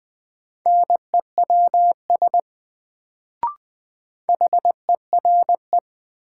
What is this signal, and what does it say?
Morse code, heard as a keyed tone, about 20 wpm, 700 Hz. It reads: NEWS HERE
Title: Sets of 5 Words - New Word Drill - 20wpm